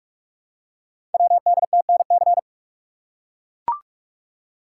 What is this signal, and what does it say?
Morse code, heard as a keyed tone, about 44 wpm, 700 Hz. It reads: WDTNC
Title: Code Groups - 5 Letters - 45wpm